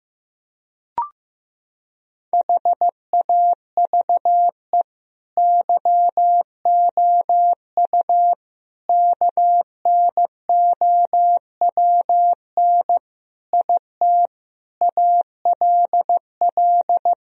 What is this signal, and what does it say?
Morse code, heard as a keyed tone, about 15 wpm, 700 Hz. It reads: HAVE YOU KNOWN IT ALL
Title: Sentences from Top 1000 Words - Review Part 3 - 15wpm